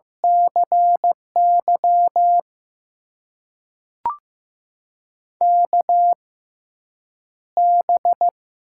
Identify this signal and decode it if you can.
Morse code, heard as a keyed tone, about 15 wpm, spaced as for 8 wpm, 700 Hz. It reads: CY KB
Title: USA Call Signs - 6 Characters - 6x Character Spacing - 15wpm